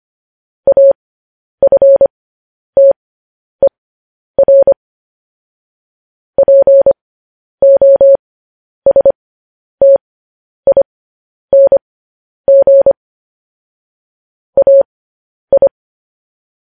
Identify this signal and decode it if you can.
Morse code, heard as a keyed tone, about 25 wpm, spaced as for 10 wpm, 550 Hz. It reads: AFTER POSTING AI
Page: news-headlines-in-morse-code-at-10-wpm